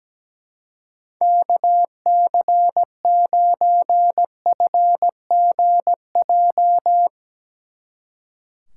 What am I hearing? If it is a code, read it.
Morse code, heard as a keyed tone, about 18 wpm, 700 Hz. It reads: KC9FGJ
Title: Speed Racing - USA Call Signs 17wpm